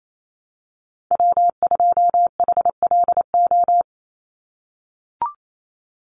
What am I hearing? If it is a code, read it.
Morse code, heard as a keyed tone, about 28 wpm, 700 Hz. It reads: W2HLO
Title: USA Call Signs 28wpm